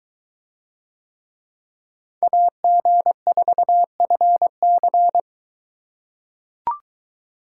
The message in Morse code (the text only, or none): AG4FC